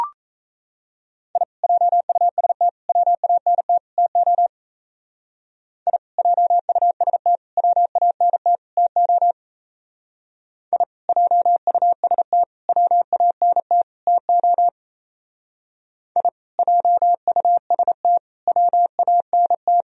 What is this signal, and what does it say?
Morse code, heard as a keyed tone, about 36 wpm, 700 Hz. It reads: I JUST WANT TO I JUST WANT TO I JUST WANT TO I JUST WANT